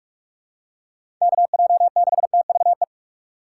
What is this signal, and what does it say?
Morse code, heard as a keyed tone, about 46 wpm, 700 Hz. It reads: KJ6TVE
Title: USA Call Signs - 6 Characters - 4x Character Spacing - 45wpm